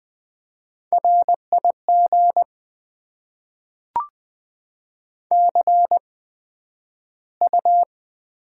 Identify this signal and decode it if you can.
Morse code, heard as a keyed tone, about 20 wpm, spaced as for 11 wpm, 700 Hz. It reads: RIG CU
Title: Top 20 QSO Elements - 8x Character Spacing - 20wpm